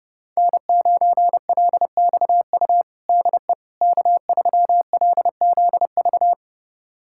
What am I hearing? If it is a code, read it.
Morse code, heard as a keyed tone, about 30 wpm, 700 Hz. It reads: N9LXU DE K3LZV
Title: USA Callsign DE Callsign - 5 Characters - 30wpm